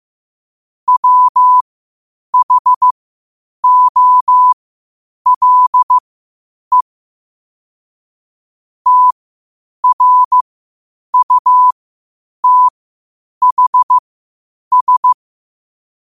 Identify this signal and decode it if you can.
Morse code, heard as a keyed tone, about 15 wpm, spaced as for 8 wpm, 1000 Hz. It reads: WHOLE TRUTHS
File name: qotdfiles / 8wpm